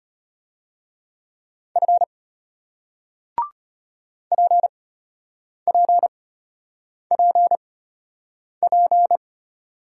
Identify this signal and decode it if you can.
Morse code, heard as a keyed tone, about 35 wpm, spaced as for 8 wpm, 700 Hz. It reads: F PPPP